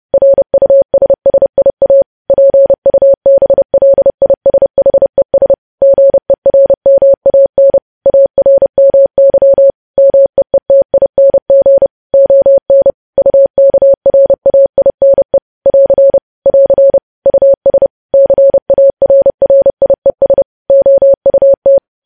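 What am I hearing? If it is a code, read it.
Morse code, heard as a keyed tone, about 30 wpm, 550 Hz. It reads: RUSSIA PUBLISHES GERMAN ARMY MEETING ON UKRAINE <AR> <AR> US CARRIES OUT
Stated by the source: Morse code practice of news headlines at 30.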